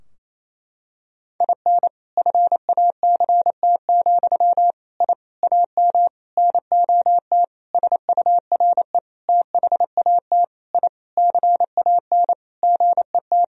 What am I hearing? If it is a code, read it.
Morse code, heard as a keyed tone, about 28 wpm, 700 Hz. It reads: IN FACT, I AM NOT SURE THAT I CAN GET
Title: Sentences from Top 1000 Words - Review Part 3 - 28wpm